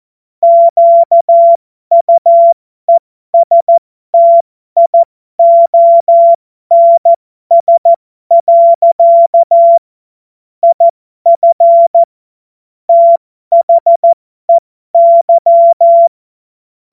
Morse code, heard as a keyed tone, about 14 wpm, spaced as for 12 wpm, 680 Hz. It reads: QUESTIONS. IF THEY
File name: Morsekurs / Witze